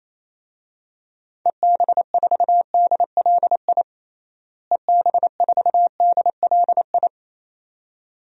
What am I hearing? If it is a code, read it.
Morse code, heard as a keyed tone, about 28 wpm, 700 Hz. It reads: EB4DLI EB4DLI
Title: DXCC-100 Call Signs - 6 Characters - Repeated 2x - 28wpm